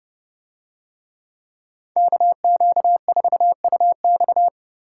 Morse code, heard as a keyed tone, about 29 wpm, 700 Hz. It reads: KQ4UX